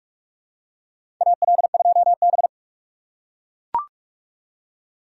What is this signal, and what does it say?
Morse code, heard as a keyed tone, about 44 wpm, 700 Hz. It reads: AL2B
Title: USA Call Signs - 4 Characters - 3x Character Spacing - 45wpm